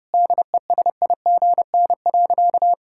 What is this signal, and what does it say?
Morse code, heard as a keyed tone, about 30 wpm, 700 Hz. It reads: DESIGN.
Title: Sentences from Top 1000 Words - Review Part 5 - 30wpm